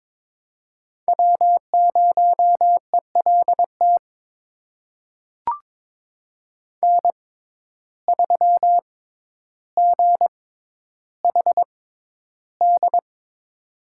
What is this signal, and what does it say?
Morse code, heard as a keyed tone, about 22 wpm, spaced as for 11 wpm, 700 Hz. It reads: W0ELT N3GHD
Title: USA Call Signs - 5 Characters - 6x Character Spacing - 22wpm